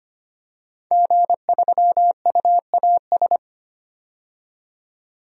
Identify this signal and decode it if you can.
Morse code, heard as a keyed tone, about 25 wpm, 700 Hz. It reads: G3UAS
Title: DXCC-100 Call Signs - 5 Characters - 25wpm